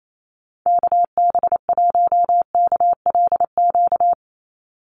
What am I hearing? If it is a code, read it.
Morse code, heard as a keyed tone, about 28 wpm, 700 Hz. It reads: KB1KLQ
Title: USA Call Signs 28wpm